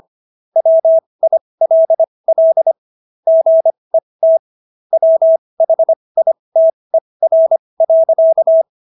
Morse code, heard as a keyed tone, about 25 wpm, spaced as for 20 wpm, 650 Hz. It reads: WILL GET WHITER.